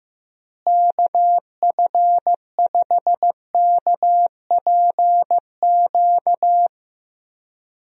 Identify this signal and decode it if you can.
Morse code, heard as a keyed tone, about 15 wpm, 700 Hz. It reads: KF5KPQ
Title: USA Call Signs - 6 Characters - Repeated 2x - 15wpm